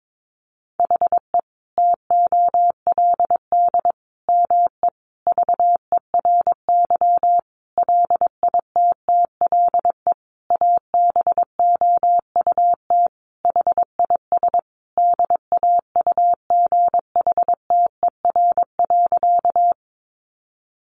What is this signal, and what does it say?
Morse code, heard as a keyed tone, about 22 wpm, 700 Hz. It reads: HE TOLD ME VERY LITTLE ABOUT HIS DAUGHTER.
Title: Sentences from Top 1500 Words 22wpm